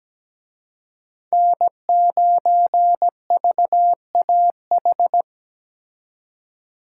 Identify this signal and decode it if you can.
Morse code, heard as a keyed tone, about 20 wpm, 700 Hz. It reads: N9VAH
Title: Speed Racing - USA Call Signs 17wpm